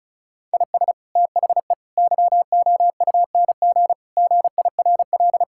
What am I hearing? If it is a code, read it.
Morse code, heard as a keyed tone, about 34 wpm, 700 Hz. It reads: IS THE YOUNG GIRL
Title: Sentences from Top 1000 Words - Review Part 5 - 35wpm